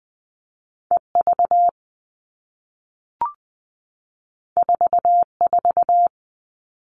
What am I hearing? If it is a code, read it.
Morse code, heard as a keyed tone, about 20 wpm, 700 Hz. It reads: E V 4 4
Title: All Cut Numbers - 2-Digit 20wpm